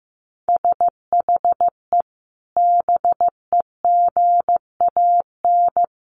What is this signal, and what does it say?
Morse code, heard as a keyed tone, about 15 wpm, 700 Hz. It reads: SHE BEGAN
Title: Sentences from Top 1200 Words 15wpm